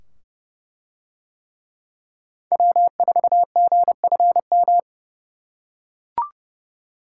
Morse code, heard as a keyed tone, about 27 wpm, 700 Hz. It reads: W4GFM